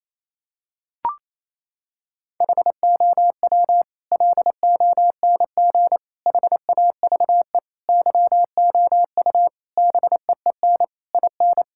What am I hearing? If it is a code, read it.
Morse code, heard as a keyed tone, about 28 wpm, 700 Hz. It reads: HOW LONG HAVE YOU BEEN IN
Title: Sentences from Top 1000 Words - Review Part 6 - 28wpm